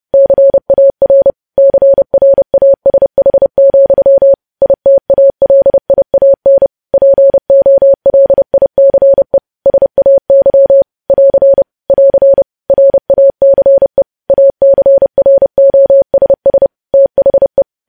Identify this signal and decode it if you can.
Morse code, heard as a keyed tone, about 30 wpm, 550 Hz. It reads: CAR CRASH, ITALIAN POLICE SAY <AR> <AR> RACE ACROSS THE
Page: news headlines in morse code at 30 wpm